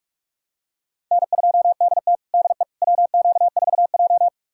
Morse code, heard as a keyed tone, about 47 wpm, 700 Hz. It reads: N2DT DE WQ4J
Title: USA Callsign DE Callsign - 4 Characters - 45wpm